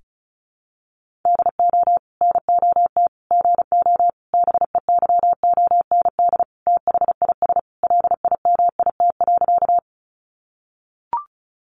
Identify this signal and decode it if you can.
Morse code, heard as a keyed tone, about 34 wpm, 700 Hz. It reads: DO NOT GO BEYOND THIS LIMIT.
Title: Sentences from Top 1500 Words 35wpm